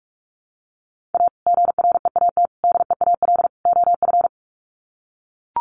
Morse code, heard as a keyed tone, about 44 wpm, 700 Hz. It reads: A GREAT DEAL OF
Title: Sets of 4 Words 45wpm